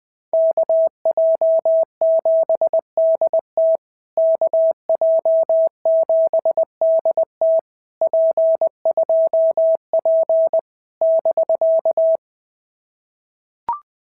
Morse code, heard as a keyed tone, about 20 wpm, 650 Hz. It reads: KJ7DT KJ7DT P2P <BK>
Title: POTA QSO Practice - 20wpm